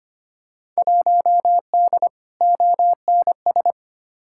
Morse code, heard as a keyed tone, about 25 wpm, 700 Hz. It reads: 1D ONS
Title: Field Day Exchanges - 25wpm